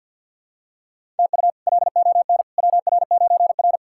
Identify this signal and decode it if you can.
Morse code, heard as a keyed tone, about 47 wpm, 700 Hz. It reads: TU LON WR9R
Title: K1USN SST QSO Practice - 50wpm